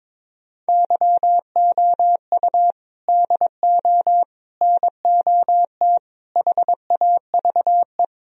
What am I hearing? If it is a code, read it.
Morse code, heard as a keyed tone, about 22 wpm, 700 Hz. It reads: YOU DO NOT HAVE
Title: Speed-Racing - Sets of 4 Words 22wpm